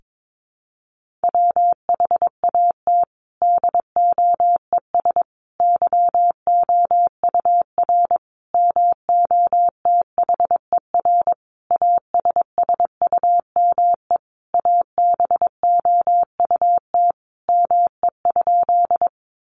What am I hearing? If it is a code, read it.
Morse code, heard as a keyed tone, about 22 wpm, 700 Hz. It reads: WHAT DOES YOUR MOTHER ASSUME ABOUT ME?
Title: Sentences from Top 1500 Words 22wpm